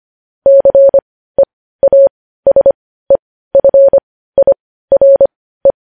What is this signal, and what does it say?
Morse code, heard as a keyed tone, about 25 wpm, spaced as for 15 wpm, 550 Hz. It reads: CEASEFIRE
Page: news headlines in morse code at 15 wpm